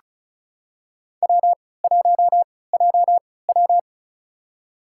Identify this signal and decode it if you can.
Morse code, heard as a keyed tone, about 36 wpm, spaced as for 20 wpm, 700 Hz. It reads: W1JW
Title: USA Call Signs - 4 Characters - 3x Character Spacing - 35wpm